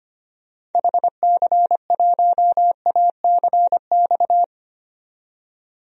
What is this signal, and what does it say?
Morse code, heard as a keyed tone, about 25 wpm, 700 Hz. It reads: HC1ACX
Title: DXCC-100 Call Signs - 6 Characters - 25wpm